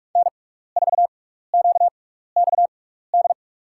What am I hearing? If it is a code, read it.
Morse code, heard as a keyed tone, about 44 wpm, spaced as for 15 wpm, 700 Hz. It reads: N4QXD
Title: USA Call Signs - 5 Characters - 6x Character Spacing - 45wpm